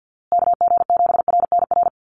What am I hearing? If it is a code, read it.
Morse code, heard as a keyed tone, about 48 wpm, 700 Hz. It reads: KG7RNR
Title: USA Call Signs Repeated 2x 50wpm